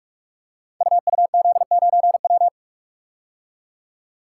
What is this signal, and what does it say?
Morse code, heard as a keyed tone, about 44 wpm, 700 Hz. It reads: UUZ9W
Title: Code Groups - 5 Letter:Numbers - 45wpm